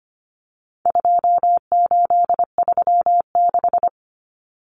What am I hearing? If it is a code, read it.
Morse code, heard as a keyed tone, about 25 wpm, 700 Hz. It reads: 2836
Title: Common Cut Numbers - 4-Digit 25wpm